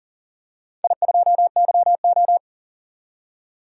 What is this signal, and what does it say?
Morse code, heard as a keyed tone, about 40 wpm, 700 Hz. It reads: I2YO
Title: DXCC-100 Call Signs - 4 Characters - 40wpm